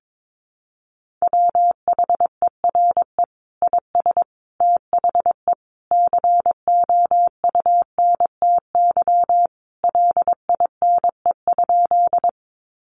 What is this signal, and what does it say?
Morse code, heard as a keyed tone, about 22 wpm, 700 Hz. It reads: WHERE IS THE COUNTY LINE?
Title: Sentences from Top 1200 Words 22wpm